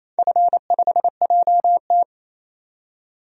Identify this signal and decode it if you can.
Morse code, heard as a keyed tone, about 28 wpm, 700 Hz. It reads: F5JT
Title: DXCC-100 Call Signs - 4 Characters - 28wpm